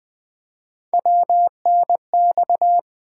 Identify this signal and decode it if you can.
Morse code, heard as a keyed tone, about 20 wpm, 700 Hz. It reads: WNX
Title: Three Letter-Number Combinations - 20wpm